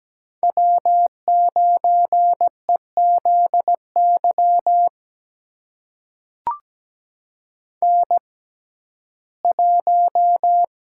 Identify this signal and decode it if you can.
Morse code, heard as a keyed tone, about 17 wpm, spaced as for 12 wpm, 700 Hz. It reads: W9EZY N1
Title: USA Call Signs - 5 Characters - 6x Character Spacing - 17wpm